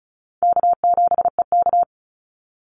Morse code, heard as a keyed tone, about 34 wpm, 700 Hz. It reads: K7EK
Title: Easy Call Signs Repeated 2x - 35wpm